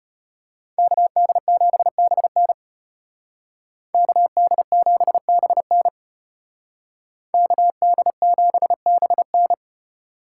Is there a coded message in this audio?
Yes: Morse code from 0.8 to 9.5 s, heard as a keyed tone, about 31 wpm, 700 Hz. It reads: KD7BN KD7BN KD7BN